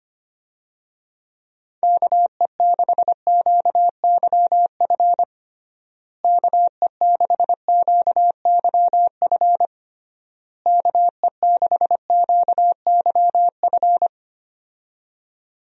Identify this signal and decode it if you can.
Morse code, heard as a keyed tone, about 25 wpm, 700 Hz. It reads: KE6QYF KE6QYF KE6QYF
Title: USA Call Signs - 6 Characters - Repeated 3x - 25wpm